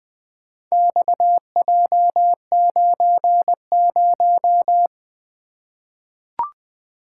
Morse code, heard as a keyed tone, about 20 wpm, 700 Hz. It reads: XJ90
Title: Four Letter-Number Combinations - 20wpm